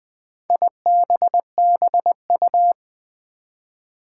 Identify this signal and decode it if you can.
Morse code, heard as a keyed tone, about 20 wpm, 700 Hz. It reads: IBBU
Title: Four Challenging Letter-Number Combinations 20wpm